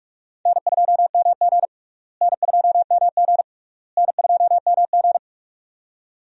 Morse code, heard as a keyed tone, about 46 wpm, 700 Hz. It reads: N2MG N2MG N2MG
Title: USA Call Signs - 4 Characters - Repeated 3x - 45wpm